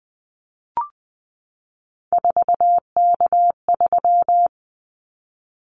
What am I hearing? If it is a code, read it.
Morse code, heard as a keyed tone, about 20 wpm, 700 Hz. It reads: 4K3
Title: Callsign Prefixes for Top 100 DXCC - 20wpm